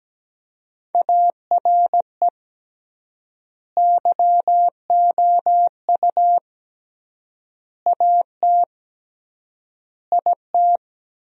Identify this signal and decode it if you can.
Morse code, heard as a keyed tone, about 17 wpm, 700 Hz. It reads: ARE YOU AT IT
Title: 3x Word-Spacing - Sentences from Top 200 Words - 17wpm